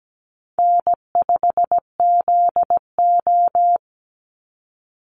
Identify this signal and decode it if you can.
Morse code, heard as a keyed tone, about 17 wpm, 700 Hz. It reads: N5ZO
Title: Easy Call Signs Repeated 2x - 17wpm